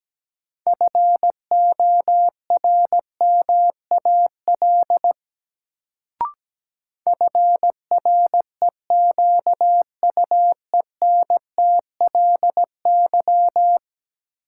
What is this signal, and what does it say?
Morse code, heard as a keyed tone, about 17 wpm, 700 Hz. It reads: FORMAL FREQUENTLY
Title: Sentences from Top 1300 Words - New Word Drill Alt - 17wpm